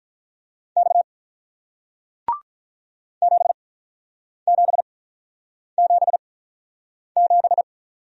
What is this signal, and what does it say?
Morse code, heard as a keyed tone, about 46 wpm, 700 Hz. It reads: X 7 7 7 7